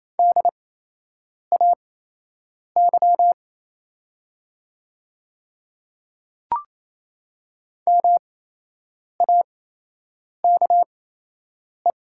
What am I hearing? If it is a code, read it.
Morse code, heard as a keyed tone, about 28 wpm, spaced as for 8 wpm, 700 Hz. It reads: DAY MAKE
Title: 8x Character Spacing - Top 100 Words - 28wpm